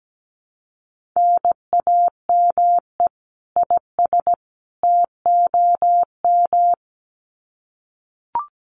Morse code, heard as a keyed tone, about 17 wpm, 700 Hz. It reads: NAME IS TOM
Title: Common QSO Phrases 17wpm